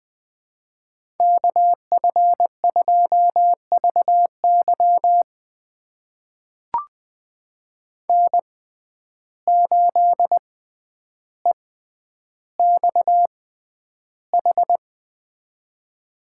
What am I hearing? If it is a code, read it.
Morse code, heard as a keyed tone, about 20 wpm, spaced as for 10 wpm, 700 Hz. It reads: KF2VY N8EXH
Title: USA Call Signs - 5 Characters - 6x Character Spacing - 20wpm